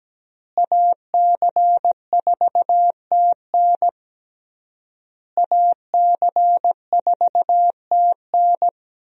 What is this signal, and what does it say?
Morse code, heard as a keyed tone, about 17 wpm, 700 Hz. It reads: AC4TN AC4TN
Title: USA Call Signs - 5 Characters - Repeated 3x - 17wpm